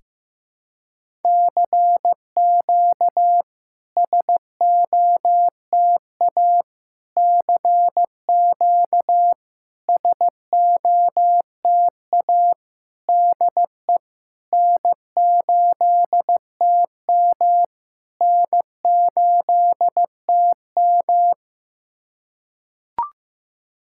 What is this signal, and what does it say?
Morse code, heard as a keyed tone, about 15 wpm, 700 Hz. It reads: CQ SOTA CQ SOTA DE N8TM N8TM
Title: SOTA QSO Practice 15wpm